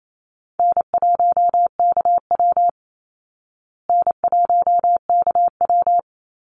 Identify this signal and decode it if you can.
Morse code, heard as a keyed tone, about 28 wpm, 700 Hz. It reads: N1KW N1KW
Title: Call Signs Repeated 2x - 28wpm